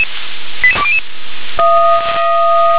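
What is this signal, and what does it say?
Five-tone selective call (ZVEI), heard as quick runs of five tones, tones of about 70 ms. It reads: 88200
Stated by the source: SDR-received ZVEI five-tone selective call.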